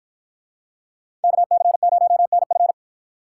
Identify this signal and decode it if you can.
Morse code, heard as a keyed tone, about 53 wpm, 700 Hz. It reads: KC0NF